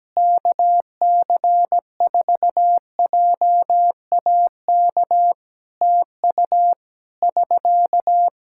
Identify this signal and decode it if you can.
Morse code, heard as a keyed tone, about 17 wpm, 700 Hz. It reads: KC4JAK TU <SK>
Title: K1USN SST QSO Practice 17wpm